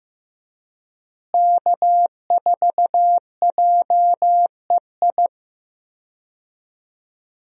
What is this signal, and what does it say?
Morse code, heard as a keyed tone, about 16 wpm, 700 Hz. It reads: K4JEI
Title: USA Call Signs - 5 Characters - Repeated 2x - 15wpm